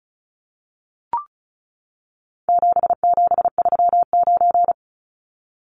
Morse code, heard as a keyed tone, about 34 wpm, 700 Hz. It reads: 7739
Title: Numbers - 4-Digit 35wpm